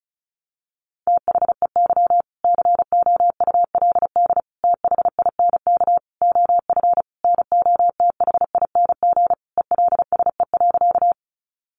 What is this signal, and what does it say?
Morse code, heard as a keyed tone, about 34 wpm, 700 Hz. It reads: THEY COULD THINK OF NOTHING ELSE.
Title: Sentences from Top 800 Words 35wpm